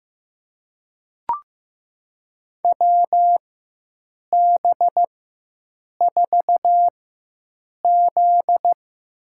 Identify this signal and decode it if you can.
Morse code, heard as a keyed tone, about 16 wpm, spaced as for 7 wpm, 700 Hz. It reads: WB4Z
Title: USA Call Signs - 4x Character Spacing - 15wpm